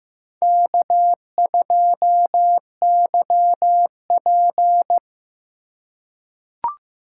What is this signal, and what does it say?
Morse code, heard as a keyed tone, about 15 wpm, 700 Hz. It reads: K2YP
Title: USA Call Signs - 4 Characters - 3x Character Spacing - 15wpm